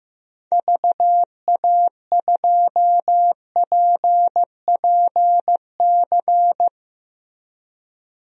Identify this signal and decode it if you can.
Morse code, heard as a keyed tone, about 15 wpm, 700 Hz. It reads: VA2PPC
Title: DXCC-100 Call Signs - 6 Characters - 15wpm